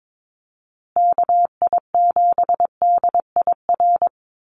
Morse code, heard as a keyed tone, about 22 wpm, 700 Hz. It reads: KI7DIR
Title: USA Call Signs Repeated 3x 22wpm